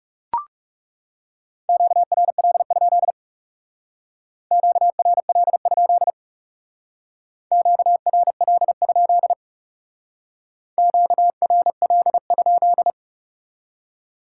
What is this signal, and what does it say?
Morse code, heard as a keyed tone, about 38 wpm, 700 Hz. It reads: QRL? QRL? QRL? QRL?